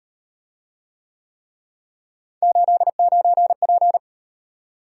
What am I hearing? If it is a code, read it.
Morse code, heard as a keyed tone, about 38 wpm, 700 Hz. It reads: 89P